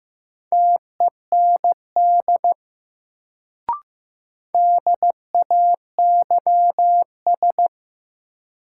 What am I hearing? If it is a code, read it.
Morse code, heard as a keyed tone, about 15 wpm, 700 Hz. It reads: TEND DAYS
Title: Sentences from Top 1000 Words - Review Part 5 - New Word Drill - 15wpm